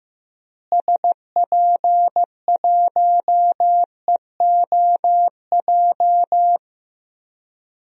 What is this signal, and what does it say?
Morse code, heard as a keyed tone, about 15 wpm, 700 Hz. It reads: SP1EOJ
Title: DXCC-100 Call Signs - 6 Characters - Repeated 3x - 15wpm